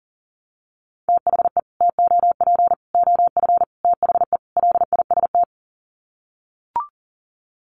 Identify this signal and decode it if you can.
Morse code, heard as a keyed tone, about 40 wpm, 700 Hz. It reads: THE TOP OF THE LIST
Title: Sets of 5 Words Encore2 40wpm